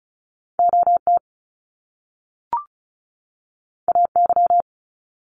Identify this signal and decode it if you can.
Morse code, heard as a keyed tone, about 36 wpm, 700 Hz. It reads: OT AY
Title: Top Two Letter Combinations in Words 35wpm